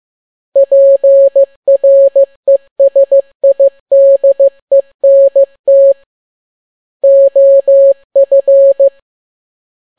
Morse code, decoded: PRESIDENT OF